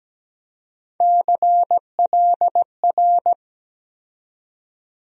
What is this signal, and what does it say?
Morse code, heard as a keyed tone, about 17 wpm, 700 Hz. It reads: CLR
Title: USA Call Sign Postfixes - 3 Letters 17wpm